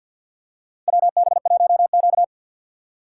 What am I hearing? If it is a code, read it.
Morse code, heard as a keyed tone, about 52 wpm, 700 Hz. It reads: WB1Q
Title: USA Call Signs - 6x Character Spacing - 50wpm